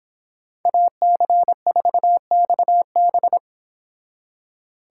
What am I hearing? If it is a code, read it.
Morse code, heard as a keyed tone, about 26 wpm, 700 Hz. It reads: AC4XB